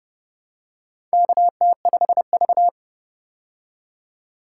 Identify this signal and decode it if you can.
Morse code, heard as a keyed tone, about 30 wpm, 700 Hz. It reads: KT5V
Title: K1USN SST QSO Practice - 30wpm